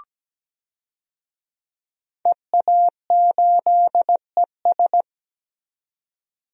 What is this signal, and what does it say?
Morse code, heard as a keyed tone, about 17 wpm, 700 Hz. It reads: EA8ES
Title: DXCC-100 Call Signs - 5 Characters - 17wpm